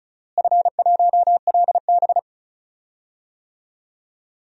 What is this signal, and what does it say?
Morse code, heard as a keyed tone, about 36 wpm, 700 Hz. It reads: F1LB
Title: DXCC-100 Call Signs - 4 Characters - Repeated 2x - 35wpm